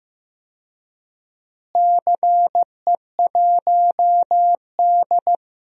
Morse code, heard as a keyed tone, about 15 wpm, 700 Hz. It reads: CE1D
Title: DXCC-100 Call Signs - 4 Characters - Repeated 3x - 15wpm